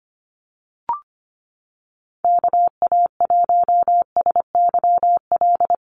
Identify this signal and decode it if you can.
Morse code, heard as a keyed tone, about 25 wpm, 700 Hz. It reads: KA1SYL
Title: USA Call Signs Repeated 3x 25wpm